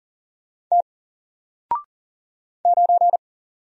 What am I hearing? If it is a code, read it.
Morse code, heard as a keyed tone, about 40 wpm, 700 Hz. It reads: T 9
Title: Single Character (Letter:Number:Punctuation:Prosign) Alt 40wpm